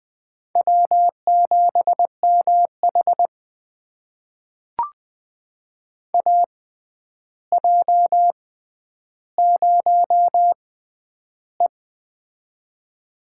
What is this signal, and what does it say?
Morse code, heard as a keyed tone, about 20 wpm, spaced as for 10 wpm, 700 Hz. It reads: W7MH AJ0E
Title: USA Call Signs - 4 Characters - 6x Character Spacing - 20wpm